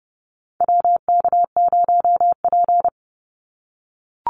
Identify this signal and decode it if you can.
Morse code, heard as a keyed tone, about 30 wpm, 700 Hz. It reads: WK0P
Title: Call Signs - 30wpm